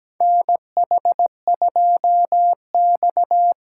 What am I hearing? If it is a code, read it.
Morse code, heard as a keyed tone, about 17 wpm, 700 Hz. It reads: NH2X
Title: USA Call Signs - 4 Characters - 2x Character Spacing - 17wpm